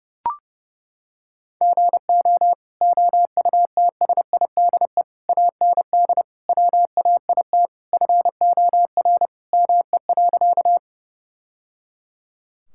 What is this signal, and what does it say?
Morse code, heard as a keyed tone, about 30 wpm, 700 Hz. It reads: GO OUTSIDE AND WAIT FOR ME.
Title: Sentences from Top 1000 Words - Review Part 6 - 30wpm